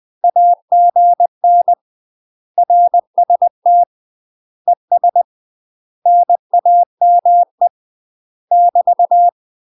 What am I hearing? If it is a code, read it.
Morse code, heard as a keyed tone, about 20 wpm, 700 Hz. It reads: AGN RST ES NAME <BT>
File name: UK MASTER QSOs TRAINING / 20WPM / UK MASTER QSOs 0003